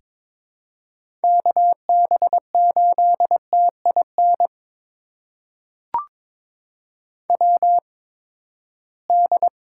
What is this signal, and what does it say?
Morse code, heard as a keyed tone, about 22 wpm, spaced as for 15 wpm, 700 Hz. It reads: KB8TIN WD
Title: USA Call Signs - 6 Characters - 8x Character Spacing - 22wpm